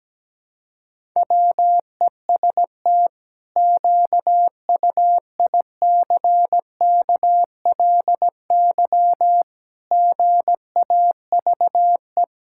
Morse code, heard as a keyed tone, about 17 wpm, 700 Hz. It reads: WEST QUICKLY GAVE
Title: Pangram Challenge v2 - 17wpm